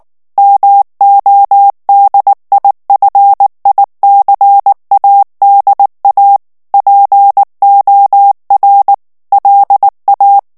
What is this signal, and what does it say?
Morse code, heard as a keyed tone, about 19 wpm, 800 Hz. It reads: MODIFICADA POR LA
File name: morse mp3 / ESPAGNOL TEXTE EN CLAIR 2 VITESSE 18 MOTS MINUTES